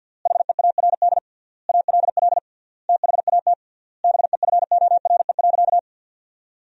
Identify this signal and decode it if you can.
Morse code, heard as a keyed tone, about 50 wpm, 700 Hz. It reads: HEARD ALL THAT BEFORE.
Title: 3x Word-Spacing - Sentences from Top 400 Words - 50wpm